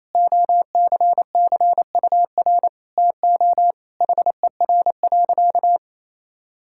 Morse code, heard as a keyed tone, about 28 wpm, 700 Hz. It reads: OCCUR TO HER.
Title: Sentences from Top 1000 Words - Review Part 6 - 28wpm